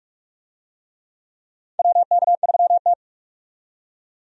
Morse code, heard as a keyed tone, about 45 wpm, 700 Hz. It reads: WK3T
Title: USA Call Signs - 4 Characters - 45wpm